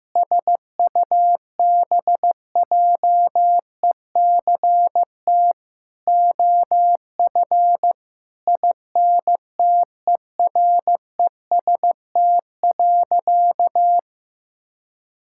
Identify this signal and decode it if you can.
Morse code, heard as a keyed tone, about 15 wpm, 690 Hz. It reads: SUBJECT OF INTEREST.
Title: Sentences from Top 1000 Words - Review Part 5 - 15wpm